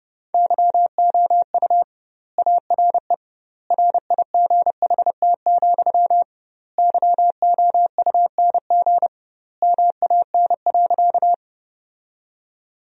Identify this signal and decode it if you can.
Morse code, heard as a keyed tone, about 30 wpm, 700 Hz. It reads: YOU ARE RIGHT, YOUNG MAN.
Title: 2x Word-Spacing - Sentences from Top 400 Words - 30wpm